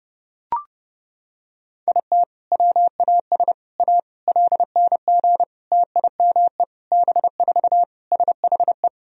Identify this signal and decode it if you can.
Morse code, heard as a keyed tone, about 32 wpm, 700 Hz. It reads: IT WAS A LNG TIME B4 SHE
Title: Sentences from Top 500 Words and Abbreviations 30wpm